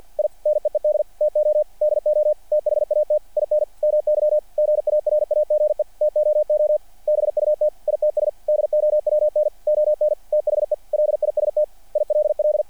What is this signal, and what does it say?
Morse code, heard as a keyed tone, about 48 wpm, 600 Hz. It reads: I NEED TO DO THAT IN MY GARAGE TOO BUT ITS DOWN ON THE LIST ILL
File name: archived / CRSnet